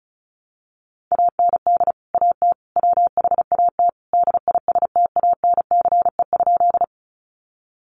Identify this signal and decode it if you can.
Morse code, heard as a keyed tone, about 34 wpm, 700 Hz. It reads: AND AT WHAT DISTANCE?
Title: Sentences from Top 1200 Words 35wpm